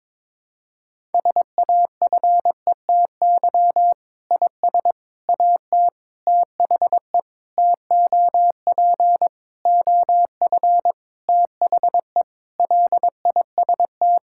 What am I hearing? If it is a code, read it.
Morse code, heard as a keyed tone, about 22 wpm, 700 Hz. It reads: SAFETY IS AT THE TOP OF THE LIST
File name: Sentences with 7-Word Phrases - 22wpm